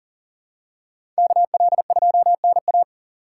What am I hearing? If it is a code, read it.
Morse code, heard as a keyed tone, about 40 wpm, 700 Hz. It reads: KL2NA
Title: USA Call Signs - 2x Character Spacing - 40wpm